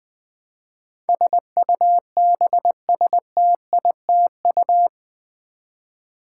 Morse code, heard as a keyed tone, about 20 wpm, 700 Hz. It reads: SUBSTITU
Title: Build A Word - 4 Letters At A Time - 12-Letter Words - 20wpm